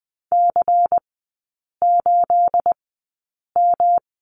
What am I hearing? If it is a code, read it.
Morse code, heard as a keyed tone, about 20 wpm, 700 Hz. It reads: C 8 M
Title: 218 Lesson 35e - Number 0 - Three Characters 20wpm